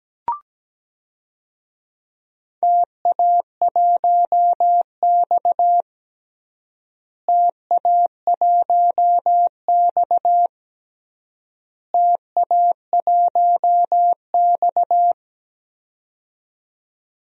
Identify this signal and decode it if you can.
Morse code, heard as a keyed tone, about 17 wpm, 700 Hz. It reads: TA1X TA1X TA1X
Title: DXCC-100 Call Signs - 4 Characters - Repeated 3x - 17wpm